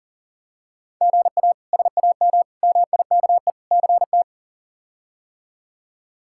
Morse code, heard as a keyed tone, about 40 wpm, 700 Hz. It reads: GA SAM MIKE CT
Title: K1USN SST QSO Practice - 40wpm